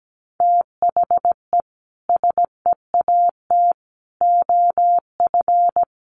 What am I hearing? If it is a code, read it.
Morse code, heard as a keyed tone, about 17 wpm, 700 Hz. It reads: THE SEAT OF
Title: Sentences from Top 1500 Words 17wpm